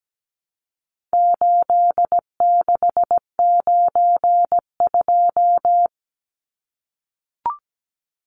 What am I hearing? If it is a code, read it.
Morse code, heard as a keyed tone, about 17 wpm, 700 Hz. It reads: 8692
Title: Numbers - 4-Digit 17wpm